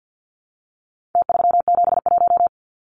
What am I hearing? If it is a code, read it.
Morse code, heard as a keyed tone, about 50 wpm, 700 Hz. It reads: T371
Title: Numbers - Contest Style - 4-Digit 50wpm